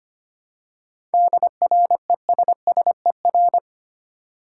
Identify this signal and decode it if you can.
Morse code, heard as a keyed tone, about 25 wpm, 690 Hz. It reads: DRESSER
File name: practice / 25wpm / Easy Words with Hints - 25wpm